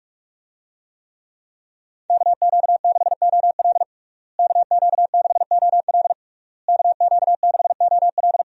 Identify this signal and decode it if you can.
Morse code, heard as a keyed tone, about 48 wpm, 700 Hz. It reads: KQ6OL KQ6OL KQ6OL
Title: USA Call Signs - 5 Characters - Repeated 3x - 45wpm